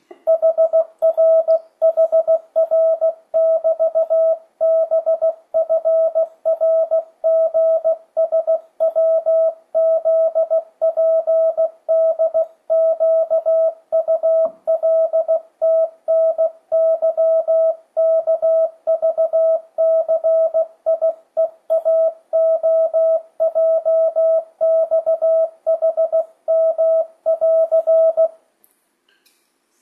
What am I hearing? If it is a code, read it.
Morse code, heard as a keyed tone, about 16 wpm, 650 Hz. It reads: HRHR<BT>BFRGSWZPDQULTNYKVCIEAOJXHM<AR>
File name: 2025cw / speedup / 16wpm